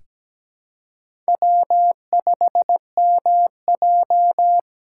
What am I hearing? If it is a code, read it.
Morse code, heard as a keyed tone, about 17 wpm, 700 Hz. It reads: W5MJ
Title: USA Call Signs - 4 Characters - 4x Character Spacing - 17wpm